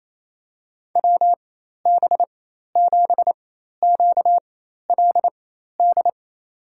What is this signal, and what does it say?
Morse code, heard as a keyed tone, about 28 wpm, spaced as for 13 wpm, 700 Hz. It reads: WB7QLD
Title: USA Call Signs - 4x Character Spacing - 28wpm